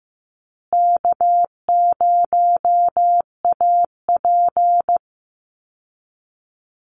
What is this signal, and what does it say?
Morse code, heard as a keyed tone, about 15 wpm, 700 Hz. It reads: K0AP
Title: Call Signs Repeated 2x - 15wpm